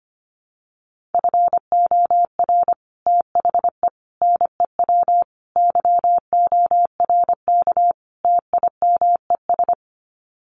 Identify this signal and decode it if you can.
Morse code, heard as a keyed tone, about 25 wpm, 700 Hz. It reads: FOR THE NEW YORK TIMES
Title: Sets of 5 Words 25wpm